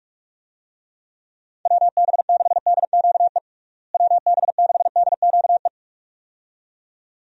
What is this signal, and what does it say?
Morse code, heard as a keyed tone, about 46 wpm, 700 Hz. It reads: WB6DQE WB6DQE
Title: USA Call Signs - 6 Characters - Repeated 2x - 45wpm